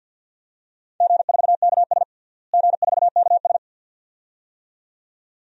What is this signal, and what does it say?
Morse code, heard as a keyed tone, about 51 wpm, 700 Hz. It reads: G4KS G4KS
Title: DXCC-100 Call Signs - 4 Characters - Repeated 2x - 50wpm